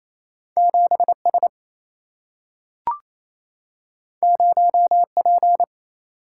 Morse code, heard as a keyed tone, about 28 wpm, 700 Hz. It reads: 7S 0P
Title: Single-Letter Single-Number 28wpm